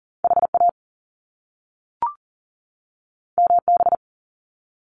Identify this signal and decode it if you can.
Morse code, heard as a keyed tone, about 40 wpm, 700 Hz. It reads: HA MB
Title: Top Two Letter Combinations in Words 40wpm